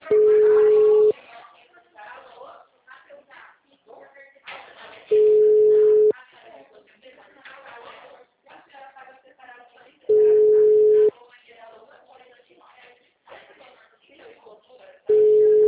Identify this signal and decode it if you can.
Ringback tone (Europe, 425 Hz), with speech.